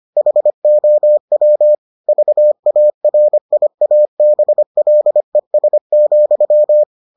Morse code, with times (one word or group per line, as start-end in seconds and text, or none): 0.2-1.7 HOW
2.1-6.8 VARIABLES,